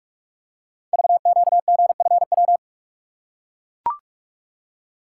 Morse code, heard as a keyed tone, about 44 wpm, 700 Hz. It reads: VQGFW
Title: Code Groups - 5 Letters - 45wpm